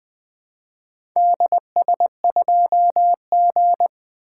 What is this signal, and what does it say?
Morse code, heard as a keyed tone, about 20 wpm, 700 Hz. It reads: DS2G
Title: Four Letter-Number Combinations - 20wpm